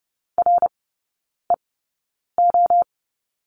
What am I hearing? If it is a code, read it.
Morse code, heard as a keyed tone, about 30 wpm, spaced as for 9 wpm, 700 Hz. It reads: REO
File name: Lesson 16e - Letter c - Three Characters 30wpm